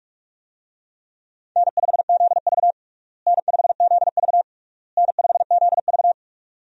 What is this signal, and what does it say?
Morse code, heard as a keyed tone, about 46 wpm, 700 Hz. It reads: N5ZV N5ZV N5ZV
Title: USA Call Signs - 4 Characters - Repeated 3x - 45wpm